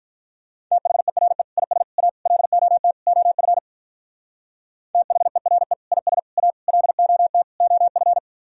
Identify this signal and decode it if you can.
Morse code, heard as a keyed tone, about 42 wpm, 700 Hz. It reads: THERE IS A LOT OF THERE IS A LOT OF
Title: Speed-Racing - Sets of 5 Words 35wpm